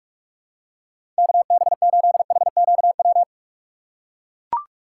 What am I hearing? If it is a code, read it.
Morse code, heard as a keyed tone, about 44 wpm, 700 Hz. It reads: KB8HQW
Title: USA Call Signs - 6 Characters - 8x Character Spacing - 45wpm